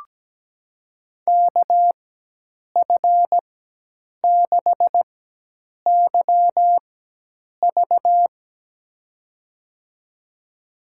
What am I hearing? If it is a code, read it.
Morse code, heard as a keyed tone, about 17 wpm, spaced as for 8 wpm, 700 Hz. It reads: KF6YV
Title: USA Call Signs - 5 Characters - 4x Character Spacing - 17wpm